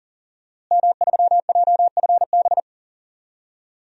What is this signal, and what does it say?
Morse code, heard as a keyed tone, about 40 wpm, 700 Hz. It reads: M3JFB
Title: DXCC-100 Call Signs - 5 Characters - Repeated 2x - 40wpm